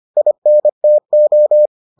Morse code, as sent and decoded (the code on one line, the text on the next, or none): .. -. - ---
INTO